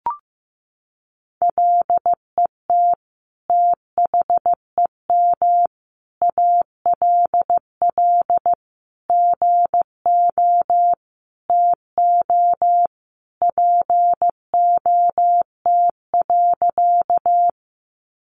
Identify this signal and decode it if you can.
Morse code, heard as a keyed tone, about 15 wpm, 700 Hz. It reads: LET THEM ALL GO TO POT.
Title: Sentences from Top 1000 Words 15wpm